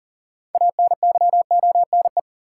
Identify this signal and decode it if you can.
Morse code, heard as a keyed tone, about 40 wpm, 700 Hz. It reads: ANYONE
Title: Sentences from Top 900 Words - New Word Drill - 40wpm